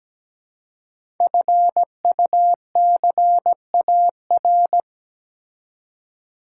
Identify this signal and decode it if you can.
Morse code, heard as a keyed tone, about 17 wpm, 700 Hz. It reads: FUCAR
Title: Code Groups - 5 Letter:Numbers - 17wpm